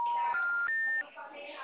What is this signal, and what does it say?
Special information tone, with speech.
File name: out- -2010- - - .10